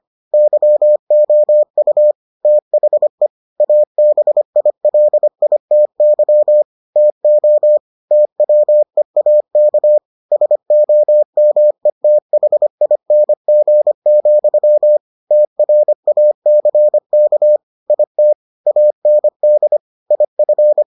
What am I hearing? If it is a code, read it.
Morse code, heard as a keyed tone, about 25 wpm, 600 Hz. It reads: YOU THE ABILITY TO TWEAK SOMETHING, TRACK IT AND IF